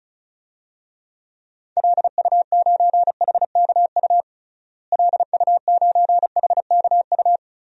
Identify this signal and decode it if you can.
Morse code, heard as a keyed tone, about 34 wpm, 700 Hz. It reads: LU9HKU LU9HKU
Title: DXCC-100 Call Signs - 6 Characters - Repeated 3x - 35wpm